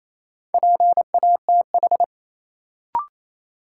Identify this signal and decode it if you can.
Morse code, heard as a keyed tone, about 28 wpm, 700 Hz. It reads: PATH
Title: Sentences from Top 1400 Words - New Word Drill Alt - 28wpm